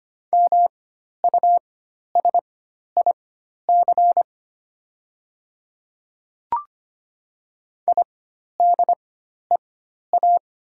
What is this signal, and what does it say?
Morse code, heard as a keyed tone, about 25 wpm, spaced as for 12 wpm, 700 Hz. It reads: MUSIC IDEA
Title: 4x Character Spacing - Top 500 Words - 25wpm